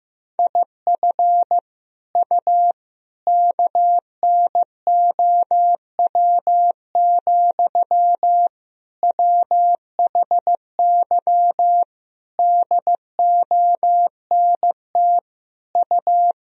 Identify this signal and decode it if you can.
Morse code, heard as a keyed tone, about 16 wpm, 700 Hz. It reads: IF U KNOW, WHY DONT U
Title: Sentences from Top 300 Words and Abbreviations 15wpm